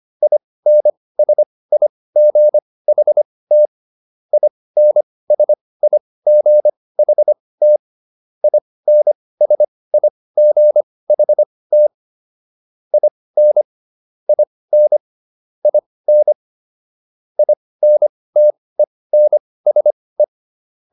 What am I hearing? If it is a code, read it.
Morse code, heard as a keyed tone, about 25 wpm, spaced as for 18 wpm, 600 Hz. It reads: INSIGHT INSIGHT INSIGHT IN IN IN INTENSE